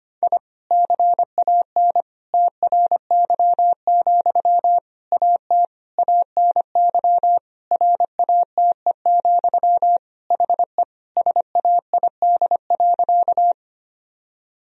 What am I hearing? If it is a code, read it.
Morse code, heard as a keyed tone, about 25 wpm, 700 Hz. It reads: I CAN TRY, AT ANY RATE, HE SAID.
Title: Sentences from Top 1000 Words - Review Part 4 - 25wpm